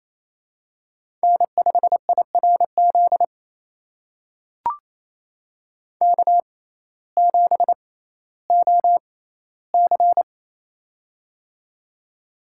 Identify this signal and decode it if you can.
Morse code, heard as a keyed tone, about 28 wpm, spaced as for 15 wpm, 700 Hz. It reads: N5IRZ K7OC
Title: USA Call Signs - 6x Character Spacing - 28wpm